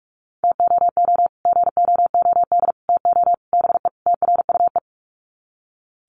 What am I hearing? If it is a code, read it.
Morse code, heard as a keyed tone, about 45 wpm, 700 Hz. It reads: TOO GOOD TO BE TRUE
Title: Sets of 5 Words Encore 45wpm